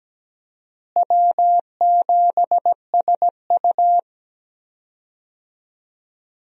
Morse code, heard as a keyed tone, about 17 wpm, 700 Hz. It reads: W7SU
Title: USA Call Signs - 4 Characters - Repeated 3x - 17wpm